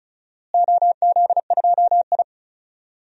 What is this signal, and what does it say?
Morse code, heard as a keyed tone, about 36 wpm, 700 Hz. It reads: OZ2I
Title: DXCC-100 Call Signs - 4 Characters - Repeated 2x - 35wpm